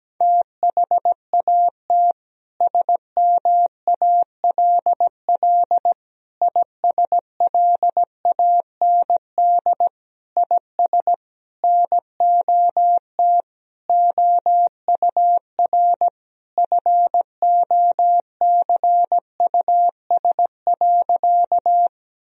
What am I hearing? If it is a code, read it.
Morse code, heard as a keyed tone, about 17 wpm, 700 Hz. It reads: THAT SMALL ISLAND IS NOT OUR FOCUS.
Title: Sentences from Top 1000 Words - Review Part 4 - 17wpm